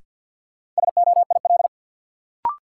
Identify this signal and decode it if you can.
Morse code, heard as a keyed tone, about 52 wpm, 700 Hz. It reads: SOIL
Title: Sentences from Top 1000 Words - Review Part 4 - New Word Drill - 50wpm